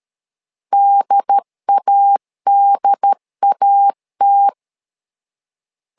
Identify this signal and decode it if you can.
Morse code, heard as a keyed tone, about 13 wpm, 800 Hz. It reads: DADAT
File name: LEZIONI / 2 UHDN / 2 UHD 12wpm 1540